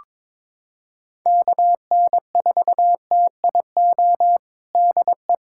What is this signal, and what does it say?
Morse code, heard as a keyed tone, about 22 wpm, 700 Hz. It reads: KN4TIO DE
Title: USA Callsign DE Callsign - 6 Characters - 22wpm